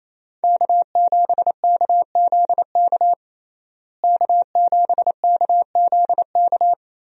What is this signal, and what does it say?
Morse code, heard as a keyed tone, about 27 wpm, 700 Hz. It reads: K7KZK K7KZK
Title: USA Call Signs - 5 Characters - Repeated 2x - 28wpm